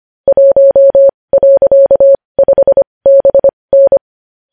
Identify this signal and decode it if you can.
Morse code, heard as a keyed tone, about 25 wpm, spaced as for 20 wpm, 550 Hz. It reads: 1.5BN
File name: podcast / News headlines-1x-25x20